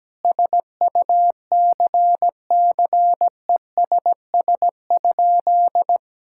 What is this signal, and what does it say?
Morse code, heard as a keyed tone, about 17 wpm, 700 Hz. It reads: SUCCESS?
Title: Sentences from Top 1000 Words - Review Part 6 - 17wpm